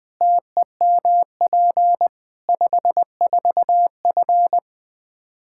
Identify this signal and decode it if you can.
Morse code, heard as a keyed tone, about 20 wpm, 700 Hz. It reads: TEMP 54F
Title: Practice QSO - Exchange 2 - Info - 20wpm